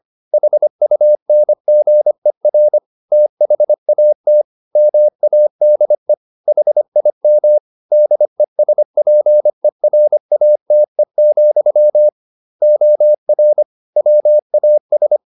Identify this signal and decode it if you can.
Morse code, heard as a keyed tone, about 25 wpm, 600 Hz. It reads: HUNGER THAT MADE HIM DESPERATE, OR WAS